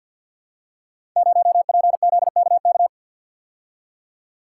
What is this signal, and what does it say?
Morse code, heard as a keyed tone, about 48 wpm, 700 Hz. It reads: 0PZKK
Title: Code Groups - 5 Letter:Numbers - 50wpm